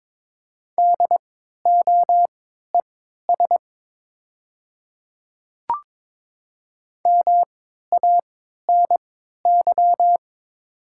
Morse code, heard as a keyed tone, about 22 wpm, spaced as for 13 wpm, 700 Hz. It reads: DOES MANY
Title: 3x Character Spacing - Top 200 Words - 22wpm